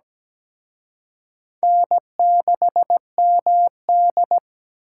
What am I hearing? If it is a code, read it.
Morse code, heard as a keyed tone, about 20 wpm, 700 Hz. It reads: N6MD
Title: Speed Racing - USA 1x2 Call Signs 17wpm